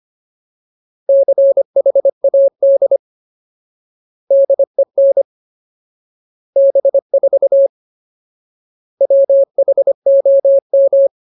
Morse code, heard as a keyed tone, about 25 wpm, 550 Hz. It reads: CHAD DEN B4 WHOM
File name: Ultimate Word List / 25WPM Ultimate